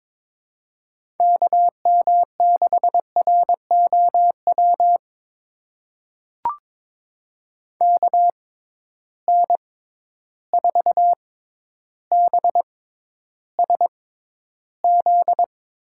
Morse code, heard as a keyed tone, about 22 wpm, spaced as for 11 wpm, 700 Hz. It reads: KM6ROW KN4BSZ
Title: USA Call Signs - 6 Characters - 6x Character Spacing - 22wpm